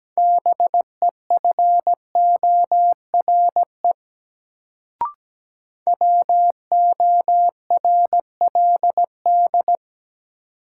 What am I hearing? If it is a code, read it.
Morse code, heard as a keyed tone, about 17 wpm, 700 Hz. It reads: BEFORE WORLD
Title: Sentences from Top 200 Words - New Word Drill - 17wpm